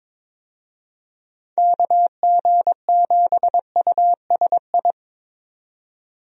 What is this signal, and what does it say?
Morse code, heard as a keyed tone, about 22 wpm, 700 Hz. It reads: KG7USI
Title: USA Call Signs - 6 Characters - 22wpm